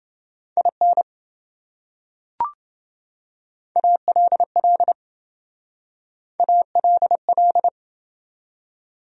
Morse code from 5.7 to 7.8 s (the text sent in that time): ALL